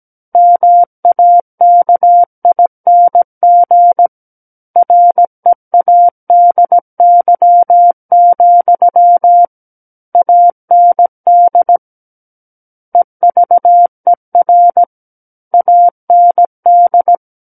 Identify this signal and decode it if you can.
Morse code, heard as a keyed tone, about 17 wpm, 700 Hz. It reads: MAKING READY, AND EVER AND
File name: Book One / 08-Chapter 8 - 17 WPM